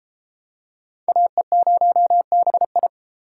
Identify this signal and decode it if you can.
Morse code, heard as a keyed tone, about 30 wpm, 700 Hz. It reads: AE0BI